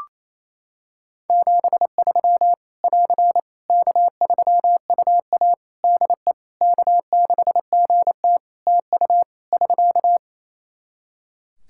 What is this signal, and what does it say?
Morse code, heard as a keyed tone, about 28 wpm, 700 Hz. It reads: 73 <AR> K3UA DE K6GT TU <SK>
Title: Practice QSO - Exchange 4 - Ending - 28wpm